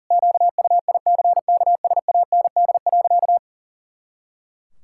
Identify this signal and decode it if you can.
Morse code, heard as a keyed tone, about 37 wpm, 700 Hz. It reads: QUICKSAND.
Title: Pangram Challenge v2 - 40wpm